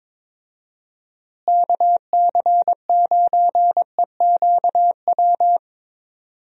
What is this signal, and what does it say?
Morse code, heard as a keyed tone, about 22 wpm, 700 Hz. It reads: KC9EQW
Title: USA Call Signs - 6 Characters - Repeated 2x - 22wpm